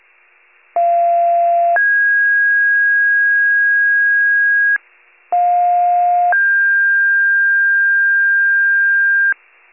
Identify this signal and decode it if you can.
Two-tone page, heard as a one-second tone, then a three-second tone, about 690 then 1650 Hz, about 710 then 1600 Hz. It is heard from 0.8 to 9.3 s.